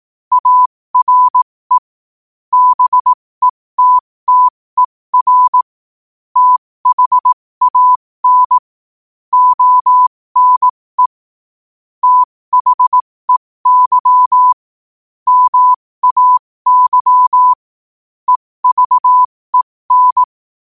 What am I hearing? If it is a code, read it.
Morse code, heard as a keyed tone, about 18 wpm, spaced as for 15 wpm, 1000 Hz. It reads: ARE BETTER THAN ONE THEY MAY EVEN
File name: qotdfiles / 15wpm